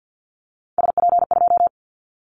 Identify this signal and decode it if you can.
Morse code, heard as a keyed tone, about 50 wpm, 700 Hz. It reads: SP2
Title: Callsign Prefixes for Top 25 DXCC - Repeated 3x - 50wpm